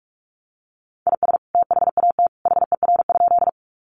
Morse code, heard as a keyed tone, about 44 wpm, 700 Hz. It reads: IS THAT HER?
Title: Sentences from Top 100 Words 45wpm